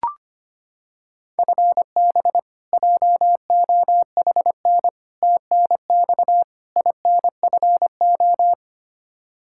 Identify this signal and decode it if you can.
Morse code, heard as a keyed tone, about 25 wpm, 700 Hz. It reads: FB JOHN TNX INFO
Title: Practice QSO - Exchange 3 - More Info - 25wpm